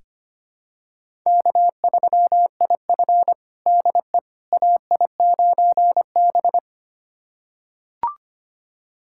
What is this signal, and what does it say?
Morse code, heard as a keyed tone, about 25 wpm, 700 Hz. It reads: K3IF DE AI9B
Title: USA Callsign DE Callsign - 4 Characters - 25wpm